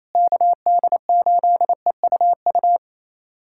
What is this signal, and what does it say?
Morse code, heard as a keyed tone, about 28 wpm, 700 Hz. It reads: KD8EUU
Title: USA Callsign DE Callsign - 6 Characters - 28wpm